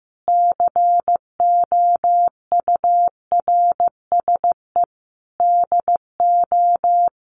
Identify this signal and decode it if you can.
Morse code, heard as a keyed tone, about 15 wpm, 700 Hz. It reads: COURSE DO
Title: Sentences from Top 1200 Words 15wpm